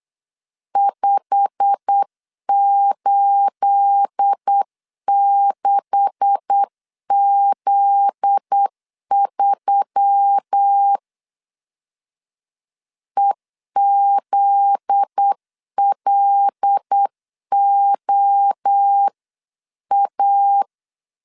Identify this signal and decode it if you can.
Morse code, heard as a keyed tone, about 8 wpm, 800 Hz. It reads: 586Z3 EZLOA
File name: CW / DET / 1 9 1500C S A 8wpm 5637